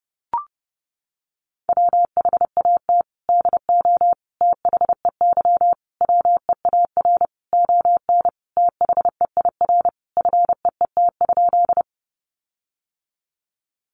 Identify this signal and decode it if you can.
Morse code, heard as a keyed tone, about 30 wpm, 700 Hz. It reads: WHAT DO THEY WEAR ON THEIR FEET?
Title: Sentences from Top 1500 Words 30wpm